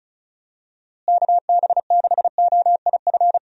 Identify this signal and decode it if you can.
Morse code, heard as a keyed tone, about 35 wpm, 700 Hz. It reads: KB6OIF
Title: USA Call Signs - 6 Characters - Repeated 2x - 35wpm